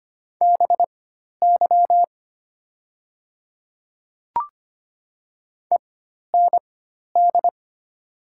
Morse code, heard as a keyed tone, about 25 wpm, spaced as for 12 wpm, 700 Hz. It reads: BY END
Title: 4x Character Spacing - Top 300 Words - 25wpm